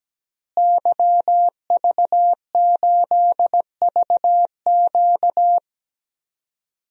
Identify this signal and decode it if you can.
Morse code, heard as a keyed tone, about 17 wpm, 700 Hz. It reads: YV8VQ
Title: DXCC-100 Call Signs - 5 Characters - Repeated 2x - 17wpm